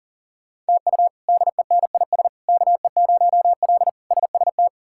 Morse code, heard as a keyed tone, about 40 wpm, 700 Hz. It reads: TU DENIS KE0L SST
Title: K1USN SST QSO Practice - 40wpm